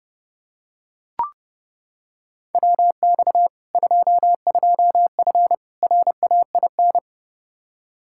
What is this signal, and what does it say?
Morse code, heard as a keyed tone, about 30 wpm, 700 Hz. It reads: WX 22F RAIN
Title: Practice QSO - Exchange 2 - Info - 30wpm